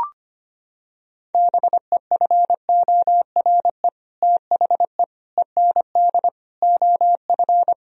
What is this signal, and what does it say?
Morse code, heard as a keyed tone, about 25 wpm, 700 Hz. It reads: BEFORE THE END OF
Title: Sentences from Top 1000 Words - Review Part 2 - 25wpm